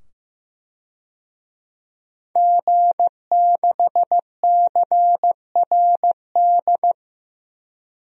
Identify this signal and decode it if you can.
Morse code, heard as a keyed tone, about 15 wpm, 700 Hz. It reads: G6CRD
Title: DXCC-100 Call Signs - 5 Characters - 15wpm